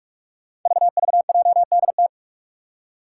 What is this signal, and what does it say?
Morse code, heard as a keyed tone, about 44 wpm, 700 Hz. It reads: VVJDT
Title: Five Challenging Letter-Number Combinations 45wpm